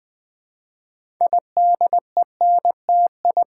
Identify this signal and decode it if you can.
Morse code, heard as a keyed tone, about 20 wpm, 700 Hz. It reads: IDENTI
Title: Build A Word - 3 Letters At A Time - 8-Letter Words - 20wpm